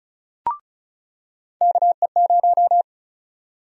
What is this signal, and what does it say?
Morse code, heard as a keyed tone, about 36 wpm, 700 Hz. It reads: KE0
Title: USA Call Sign Prefixes - 35wpm